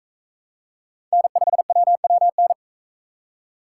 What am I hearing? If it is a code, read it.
Morse code, heard as a keyed tone, about 41 wpm, 700 Hz. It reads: N5WWN